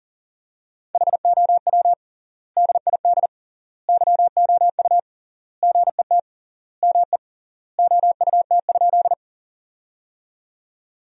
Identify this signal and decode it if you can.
Morse code, heard as a keyed tone, about 40 wpm, 700 Hz. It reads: HOW DID YOU GET ME OUT?
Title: 3x Word-Spacing - Sentences from Top 200 Words - 40wpm